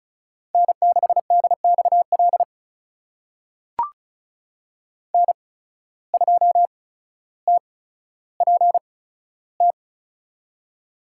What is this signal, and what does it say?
Morse code, heard as a keyed tone, about 36 wpm, spaced as for 15 wpm, 700 Hz. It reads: N6DXL N2TPT
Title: USA Call Signs - 5 Characters - 8x Character Spacing - 35wpm